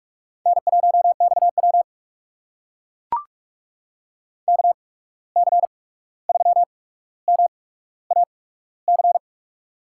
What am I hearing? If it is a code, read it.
Morse code, heard as a keyed tone, about 44 wpm, spaced as for 17 wpm, 700 Hz. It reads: N1XW KC3MAC
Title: USA Call Signs - 8x Character Spacing - 45wpm